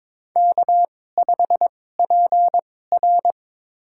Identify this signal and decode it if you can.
Morse code, heard as a keyed tone, about 22 wpm, spaced as for 16 wpm, 700 Hz. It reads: K5PR
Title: USA Call Signs - 4 Characters - 2x Character Spacing - 22wpm